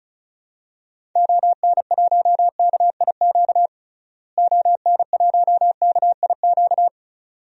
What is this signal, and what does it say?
Morse code, heard as a keyed tone, about 36 wpm, 700 Hz. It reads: ON1KIQ ON1KIQ
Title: DXCC-100 Call Signs - 6 Characters - Repeated 2x - 35wpm